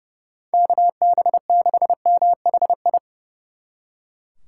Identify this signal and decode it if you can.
Morse code, heard as a keyed tone, about 36 wpm, 700 Hz. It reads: KB6MHI
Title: Speed Racing - USA 2x3 Call Signs 30wpm